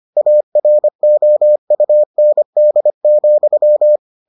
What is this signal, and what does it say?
Morse code, heard as a keyed tone, about 25 wpm, 600 Hz. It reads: AROUND,